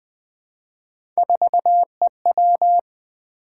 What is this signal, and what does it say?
Morse code, heard as a keyed tone, about 20 wpm, 700 Hz. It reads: 4EW
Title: Three Letter-Number Combinations - 20wpm